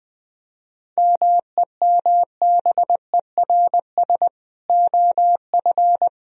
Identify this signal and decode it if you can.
Morse code, heard as a keyed tone, about 20 wpm, 700 Hz. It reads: MEMBERS OF
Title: Speed-Racing - Sets of 3 Words 17wpm